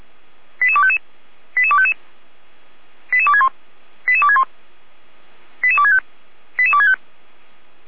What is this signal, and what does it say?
Five-tone selective call (ZVEI), heard as quick runs of five tones, tones of about 70 ms. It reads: 88255 88255 88261 88261 88265 88265
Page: SDR/ZVEI